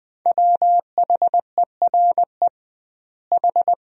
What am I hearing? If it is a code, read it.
Morse code, heard as a keyed tone, about 21 wpm, 700 Hz. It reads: WHERE H
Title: 2x Word-Spacing - Sentences from Top 300 Words - 20wpm